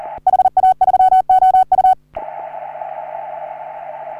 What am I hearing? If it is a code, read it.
Morse code, heard as a keyed tone, about 38 wpm, 750 Hz. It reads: HA3OU
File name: CW / HA3OU RHOBCW REC